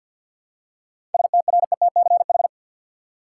Morse code, heard as a keyed tone, about 50 wpm, 700 Hz. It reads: STRETCH
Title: Seven Letter Words - 50wpm